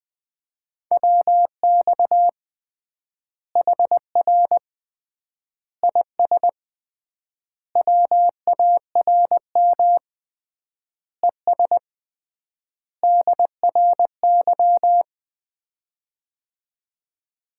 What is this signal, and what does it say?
Morse code, heard as a keyed tone, about 21 wpm, 700 Hz. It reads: WX HR IS WARM ES DRY
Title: Common QSO Phrases - 3x Word-Spacing - 20wpm